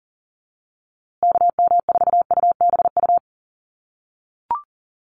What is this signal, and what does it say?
Morse code, heard as a keyed tone, about 40 wpm, 700 Hz. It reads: KM4UBU
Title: USA Call Signs Repeated 3x 40wpm